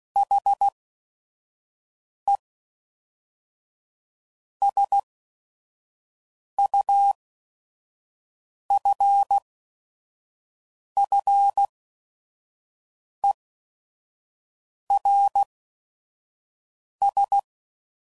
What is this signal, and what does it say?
Morse code, heard as a keyed tone, about 16 wpm, spaced as for 5 wpm, 800 Hz. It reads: HE SUFFERS